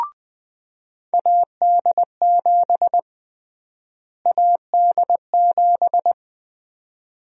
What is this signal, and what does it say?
Morse code, heard as a keyed tone, about 20 wpm, 700 Hz. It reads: AD7 AD7
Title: USA Call Sign Prefixes - Repeated 2x - 20wpm